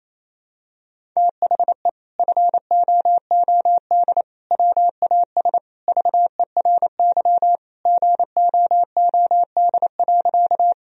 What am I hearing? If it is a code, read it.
Morse code, heard as a keyed tone, about 28 wpm, 700 Hz. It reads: THE FOOD WAS VERY GOOD.
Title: Sentences from Top 1000 Words - Review Part 1 - 28wpm